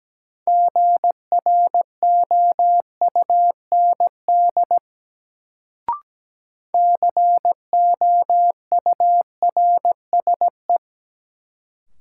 Morse code, heard as a keyed tone, about 17 wpm, 700 Hz. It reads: GROUND COURSE
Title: Sentences from Top 500 Words - New Word Drill - 17wpm